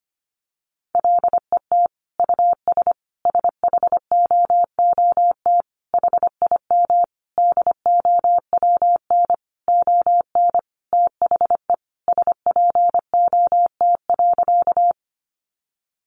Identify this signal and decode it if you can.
Morse code, heard as a keyed tone, about 25 wpm, 700 Hz. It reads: LET US SHOOT HIM DOWN ON THE SPOT.
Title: Sentences from Top 1200 Words 25wpm